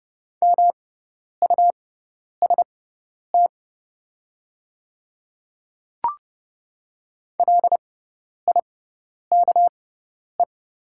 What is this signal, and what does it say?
Morse code, heard as a keyed tone, about 30 wpm, spaced as for 10 wpm, 700 Hz. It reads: MUST LIKE
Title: 6x Character Spacing - Top 500 Words - 30wpm